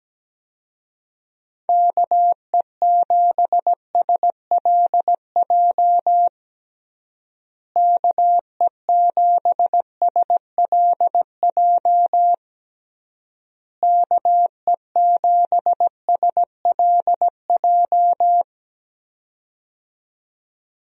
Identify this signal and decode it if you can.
Morse code, heard as a keyed tone, about 17 wpm, 700 Hz. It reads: KE7SLJ KE7SLJ KE7SLJ
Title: USA Call Signs - 6 Characters - Repeated 3x - 17wpm